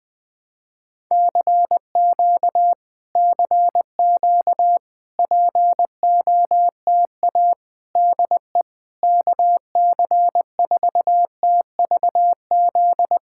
Morse code, heard as a keyed tone, about 20 wpm, 690 Hz. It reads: CQ CQ POTA DE KC4TVZ
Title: POTA QSO Practice - 20wpm